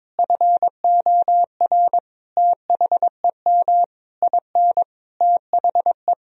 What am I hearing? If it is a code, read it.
Morse code, heard as a keyed tone, about 22 wpm, 700 Hz. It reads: FOR THEM IN THE
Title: Sentences from Top 1000 Words - Review Part 2 - 22wpm